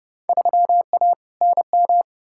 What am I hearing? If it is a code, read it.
Morse code, heard as a keyed tone, about 32 wpm, 700 Hz. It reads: 3A NM
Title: Field Day Exchanges Encore - 30wpm